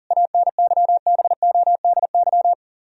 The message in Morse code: ANYBODY

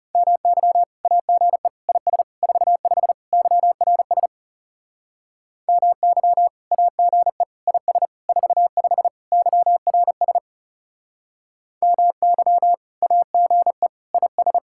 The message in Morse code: MY AGE IS 45 YRS MY AGE IS 45 YRS MY AGE IS

